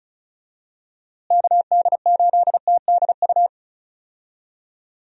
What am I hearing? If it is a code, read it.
Morse code, heard as a keyed tone, about 34 wpm, 700 Hz. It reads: KD8TDU
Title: USA Call Signs - 6 Characters - Repeated 3x - 35wpm